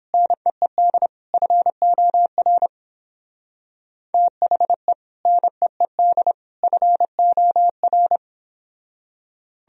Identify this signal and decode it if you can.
Morse code, heard as a keyed tone, about 31 wpm, 700 Hz. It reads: NEED FOR THE NEED FOR